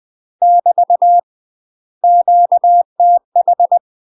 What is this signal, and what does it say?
Morse code, heard as a keyed tone, about 20 wpm, 700 Hz. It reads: <BT> QTH
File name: UK MASTER QSOs TRAINING / 20WPM / UK MASTER QSOs 0061